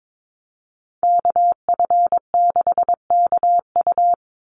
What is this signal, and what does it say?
Morse code, heard as a keyed tone, about 22 wpm, 700 Hz. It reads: KF6KU
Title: USA Call Signs 22wpm